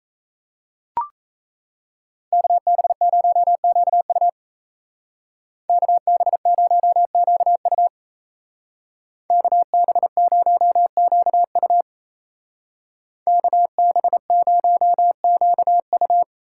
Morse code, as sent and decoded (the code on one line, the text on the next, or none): -.- -... ----- --.- ..- / -.- -... ----- --.- ..- / -.- -... ----- --.- ..- / -.- -... ----- --.- ..-
KB0QU KB0QU KB0QU KB0QU